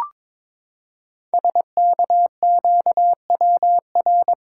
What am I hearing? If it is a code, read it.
Morse code, heard as a keyed tone, about 22 wpm, 700 Hz. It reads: SKQWR
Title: Code Groups - 5 Letter:Numbers - 22wpm